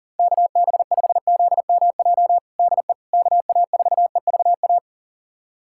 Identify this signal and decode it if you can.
Morse code, heard as a keyed tone, about 37 wpm, 700 Hz. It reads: KB5ZMJ DE KA4EVA
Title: USA Callsign DE Callsign - 6 Characters - 40wpm